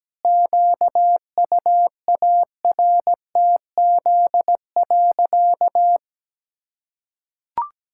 Morse code, heard as a keyed tone, about 17 wpm, 700 Hz. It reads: QUARTZ.
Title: Pangram Challenge v2 - 17wpm